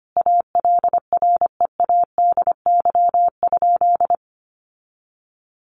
Morse code, heard as a keyed tone, about 25 wpm, 700 Hz. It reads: ALREADY?
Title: Sentences from Top 700 Words 25wpm